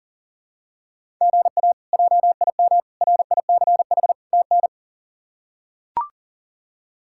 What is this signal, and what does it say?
Morse code, heard as a keyed tone, about 40 wpm, 700 Hz. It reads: GA JIM RICH TN
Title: K1USN SST QSO Practice - 40wpm